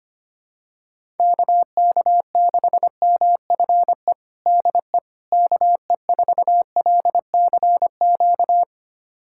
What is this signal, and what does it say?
Morse code, heard as a keyed tone, about 25 wpm, 700 Hz. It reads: KK6MFE DE KE4LCQ
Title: USA Callsign DE Callsign - 6 Characters - 25wpm